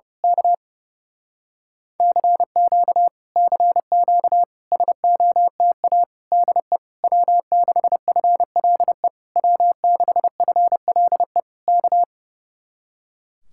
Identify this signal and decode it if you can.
Morse code, heard as a keyed tone, about 28 wpm, 700 Hz. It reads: K CQ CQ SOTA DE W6FLE W6FLE K
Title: Speed-Racing Common QSO Phrases Encore - 30wpm